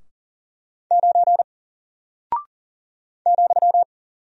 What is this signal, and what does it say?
Morse code, heard as a keyed tone, about 40 wpm, 700 Hz. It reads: 9 ,
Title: Single Character (Letter:Number:Punctuation:Prosign) Alt 40wpm